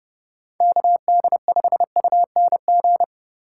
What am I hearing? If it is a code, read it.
Morse code, heard as a keyed tone, about 28 wpm, 700 Hz. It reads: KD5UNG